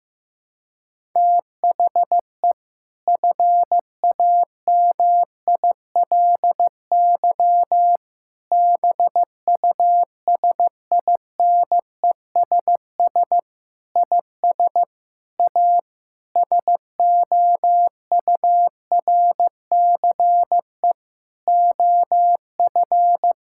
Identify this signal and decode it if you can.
Morse code, heard as a keyed tone, about 15 wpm, 700 Hz. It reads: THE FAMILY BUSINESS IS A SOURCE OF
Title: Sentences from Top 1000 Words - Review Part 6 - 15wpm